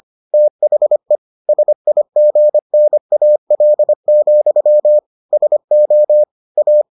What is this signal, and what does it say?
Morse code, heard as a keyed tone, about 25 wpm, 600 Hz. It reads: THE SIGNAL, SO A